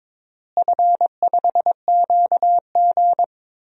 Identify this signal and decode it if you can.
Morse code, heard as a keyed tone, about 22 wpm, 700 Hz. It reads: F5QG
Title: DXCC-100 Call Signs - 4 Characters - Repeated 2x - 22wpm